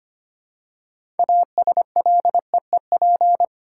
Morse code, heard as a keyed tone, about 25 wpm, 700 Hz. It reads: ASLEEP
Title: Six Letter Words - 25wpm